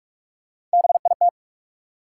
Morse code, heard as a keyed tone, about 46 wpm, 700 Hz. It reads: BIT
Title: Sets of 3 Words - New Word Drill - 45wpm